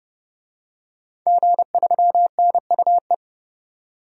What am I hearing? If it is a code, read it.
Morse code, heard as a keyed tone, about 29 wpm, 700 Hz. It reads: G3NUE
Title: DXCC-100 Call Signs - 5 Characters - 30wpm